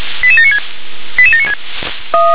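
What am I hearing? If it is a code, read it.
Five-tone selective call (ZVEI), heard as quick runs of five tones, tones of about 70 ms. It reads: 88796 88796
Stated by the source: ZVEI five-tone selective call received on an SDR.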